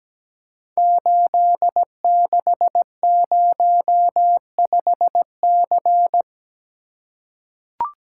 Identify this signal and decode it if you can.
Morse code, heard as a keyed tone, about 17 wpm, 700 Hz. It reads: 8605C
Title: Code Groups - 5 Letter:Numbers - 17wpm